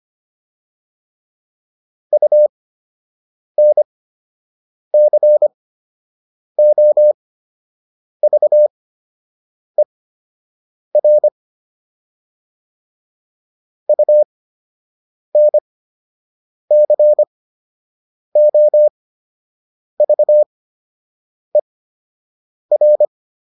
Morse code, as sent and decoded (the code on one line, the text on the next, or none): ..- -. -.-. --- ...- . .-. / ..- -. -.-. --- ...- . .-.
UNCOVER UNCOVER